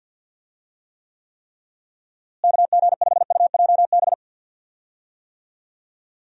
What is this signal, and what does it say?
Morse code, heard as a keyed tone, about 51 wpm, 700 Hz. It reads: KG5UJB
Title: USA Call Signs - 6 Characters - Repeated 3x - 50wpm